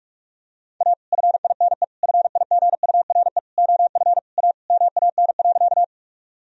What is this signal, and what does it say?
Morse code, heard as a keyed tone, about 42 wpm, 700 Hz. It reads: A FINE FIGURE OF A MAN.
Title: Sentences from Top 1000 Words - Review Part 2 - 45wpm